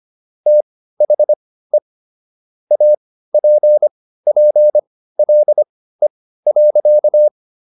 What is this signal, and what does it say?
Morse code, heard as a keyed tone, about 25 wpm, spaced as for 15 wpm, 600 Hz. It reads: THE APPLE.